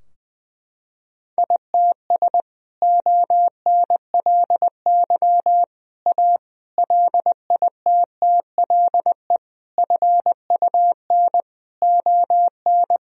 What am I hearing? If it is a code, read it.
Morse code, heard as a keyed tone, about 19 wpm, 700 Hz. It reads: ITS ONLY A LITTLE FUN ON
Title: Sentences from Top 1000 Words - Review Part 4 - 20wpm